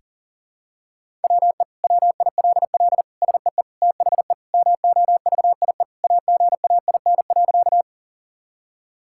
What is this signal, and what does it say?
Morse code, heard as a keyed tone, about 40 wpm, 700 Hz. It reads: WE WILL SEE THE MOVIE AGAIN.
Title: Sentences from Top 1000 Words - Review Part 5 - 40wpm